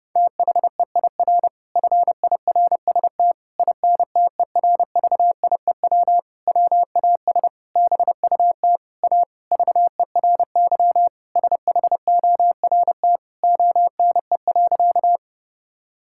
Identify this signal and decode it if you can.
Morse code, heard as a keyed tone, about 30 wpm, 700 Hz. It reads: THEIR FIRST INTERVIEW WAS BUT A VERY SHORT ONE.
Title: Sentences from Top 1600 Words 30wpm